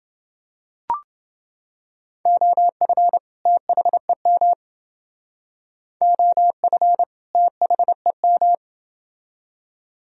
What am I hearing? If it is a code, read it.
Morse code, heard as a keyed tone, about 31 wpm, 700 Hz. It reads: OF THEM OF THEM